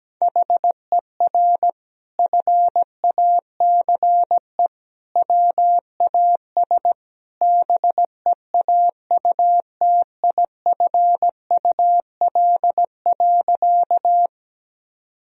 Morse code, heard as a keyed tone, about 17 wpm, 700 Hz. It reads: HER FACE WAS BEAUTIFUL.
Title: Sentences from Top 1000 Words - Review Part 2 - 17wpm